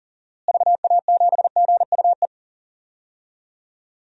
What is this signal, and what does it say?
Morse code, heard as a keyed tone, about 40 wpm, 700 Hz. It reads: VA7GUE
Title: DXCC-100 Call Signs - 6 Characters - Repeated 3x - 40wpm